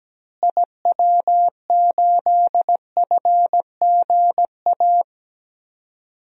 Morse code, heard as a keyed tone, about 17 wpm, 700 Hz. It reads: IW8FGA
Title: DXCC-100 Call Signs - 6 Characters - Repeated 3x - 17wpm